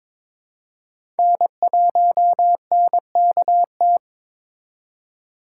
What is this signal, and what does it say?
Morse code, heard as a keyed tone, about 22 wpm, 700 Hz. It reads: N1NKT
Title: USA Call Signs - 5 Characters - 3x Character Spacing - 22wpm